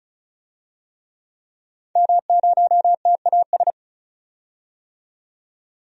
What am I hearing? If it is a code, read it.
Morse code, heard as a keyed tone, about 36 wpm, 700 Hz. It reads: M0TAS
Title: DXCC-100 Call Signs - 5 Characters - 35wpm